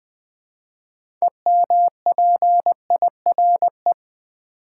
Morse code, heard as a keyed tone, about 20 wpm, 700 Hz. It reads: EMPIRE
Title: 6 Letter Words - First 3 Letters Given - 20wpm